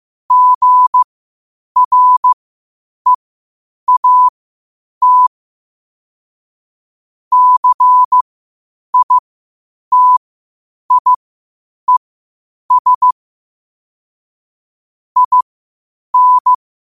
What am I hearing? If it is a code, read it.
Morse code, heard as a keyed tone, about 15 wpm, spaced as for 8 wpm, 1000 Hz. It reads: GREAT CITIES IN
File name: qotdfiles / 8wpm